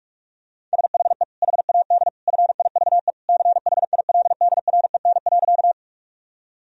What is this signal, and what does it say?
Morse code, heard as a keyed tone, about 46 wpm, 700 Hz. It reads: SHE HAD FIVE CHILDREN.
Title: Sentences from Top 1000 Words - Review Part 2 - 45wpm